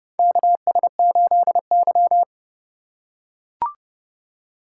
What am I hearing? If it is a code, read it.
Morse code, heard as a keyed tone, about 30 wpm, 700 Hz. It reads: KS8Y
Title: USA Call Signs - 4 Characters - 2x Character Spacing - 30wpm